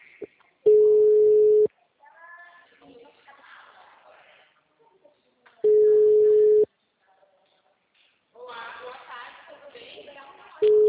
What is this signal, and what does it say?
Ringback tone (Europe, 425 Hz), with speech.